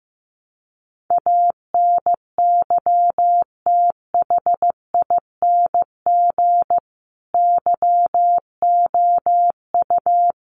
Morse code, heard as a keyed tone, about 15 wpm, 700 Hz. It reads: ANYTHING YOU
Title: Sentences from Top 1000 Words 15wpm